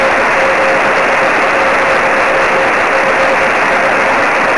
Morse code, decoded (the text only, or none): none